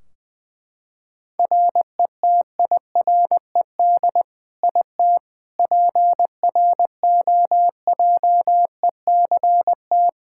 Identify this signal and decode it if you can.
Morse code, heard as a keyed tone, about 20 wpm, 700 Hz. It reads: RETIRED IT PROJECT
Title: Practice QSO - Exchange 3 - More Info - 20wpm